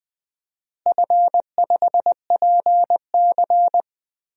Morse code, heard as a keyed tone, about 21 wpm, 700 Hz. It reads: F5PC
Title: DXCC-100 Call Signs - 4 Characters - Repeated 3x - 20wpm